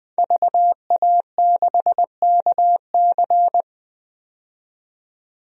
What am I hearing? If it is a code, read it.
Morse code, heard as a keyed tone, about 21 wpm, 700 Hz. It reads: VA6KC
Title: DXCC-100 Call Signs - 5 Characters - 20wpm